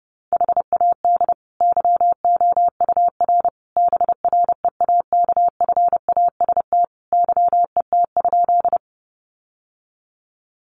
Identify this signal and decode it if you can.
Morse code, heard as a keyed tone, about 28 wpm, 700 Hz. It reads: HAD YOUR BREAKFAST YET?
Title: Sentences from Top 1500 Words 30wpm